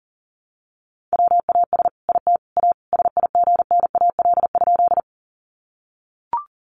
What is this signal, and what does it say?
Morse code, heard as a keyed tone, about 40 wpm, 700 Hz. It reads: WAS IT A SIGNAL?
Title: Sentences from Top 1400 Words 40wpm